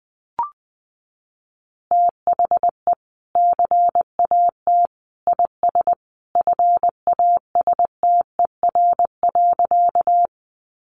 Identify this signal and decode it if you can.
Morse code, heard as a keyed tone, about 20 wpm, 700 Hz. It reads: THE CAT IS FASTER.
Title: Additional Easy Phrases 20wpm